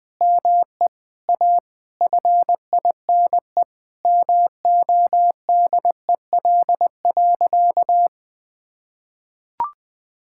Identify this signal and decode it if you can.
Morse code, heard as a keyed tone, about 20 wpm, 700 Hz. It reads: ME A FINE MODEL.
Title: Sentences from Top 1000 Words - Review Part 5 - 20wpm